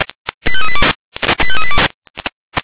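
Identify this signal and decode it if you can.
Five-tone selective call (ZVEI), heard as quick runs of five tones, tones of about 70 ms. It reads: 85382 85382
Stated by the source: ZVEI five-tone call received on SDR.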